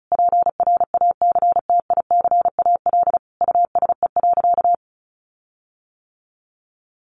Morse code, heard as a keyed tone, about 34 wpm, 700 Hz. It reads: PRACTICAL USE.
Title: Sentences from Top 1400 Words 35wpm